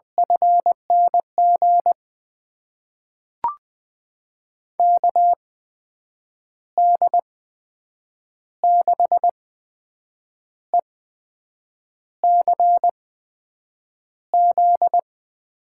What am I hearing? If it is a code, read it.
Morse code, heard as a keyed tone, about 20 wpm, spaced as for 7 wpm, 700 Hz. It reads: FNG KD6ECZ
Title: USA Call Signs - 6 Characters - 8x Character Spacing - 20wpm